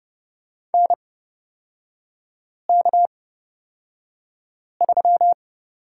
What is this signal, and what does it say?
Morse code, heard as a keyed tone, about 29 wpm, spaced as for 5 wpm, 690 Hz. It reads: NK3
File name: Lesson 38e - Letter q - Three Characters 30wpm